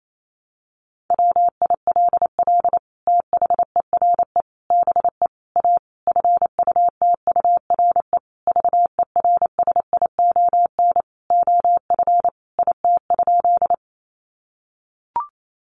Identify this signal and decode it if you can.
Morse code, heard as a keyed tone, about 28 wpm, 700 Hz. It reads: WILL THERE BE A FUTURE VERSION OF IT?
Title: Sentences from Top 1400 Words 28wpm